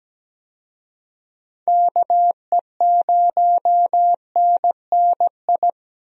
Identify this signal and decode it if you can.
Morse code, heard as a keyed tone, about 17 wpm, 700 Hz. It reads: KE0NNI
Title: USA Call Signs - 6 Characters - 17wpm